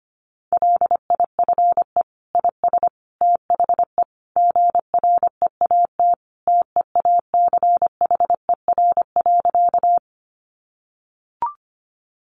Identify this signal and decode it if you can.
Morse code, heard as a keyed tone, about 25 wpm, 700 Hz. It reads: LIFE IS THE GREAT TEACHER.
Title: Sentences from Top 900 Words 25wpm